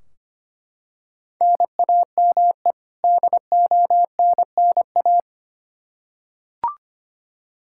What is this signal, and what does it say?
Morse code, heard as a keyed tone, about 25 wpm, 700 Hz. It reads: NAME DONNA
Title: Practice QSO - Exchange 1 - Beginning - 25wpm